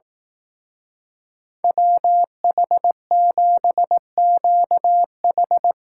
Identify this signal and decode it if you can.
Morse code, heard as a keyed tone, about 18 wpm, 700 Hz. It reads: WH7QH
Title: Speed Racing - USA 2x2 Call Signs 15wpm